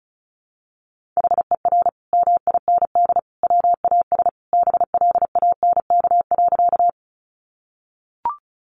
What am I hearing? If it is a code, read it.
Morse code, heard as a keyed tone, about 34 wpm, 700 Hz. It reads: HER MIND WAS BLANK.
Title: Sentences from Top 1500 Words 35wpm